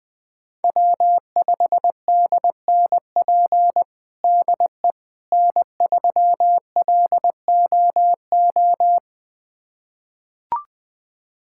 Morse code, heard as a keyed tone, about 20 wpm, 700 Hz. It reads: W5DNP DE N3LOO
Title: USA Callsign DE Callsign - 5 Characters - 20wpm